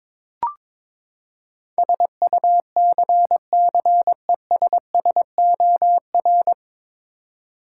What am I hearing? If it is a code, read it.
Morse code, heard as a keyed tone, about 21 wpm, 700 Hz. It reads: SUCCESSOR
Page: Morse Code Ninja practice 22wpm